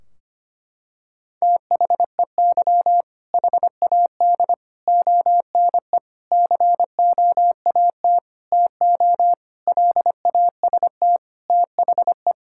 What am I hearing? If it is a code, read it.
Morse code, heard as a keyed tone, about 25 wpm, 700 Hz. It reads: THEY HAD ONE COAT TO LAST THE
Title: Sentences from Top 1600 Words 25wpm